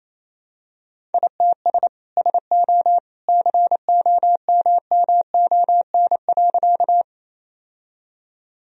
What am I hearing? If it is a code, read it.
Morse code, heard as a keyed tone, about 28 wpm, 700 Hz. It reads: ITS SO COMMON.
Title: Sentences from Top 1000 Words - Review Part 3 - 28wpm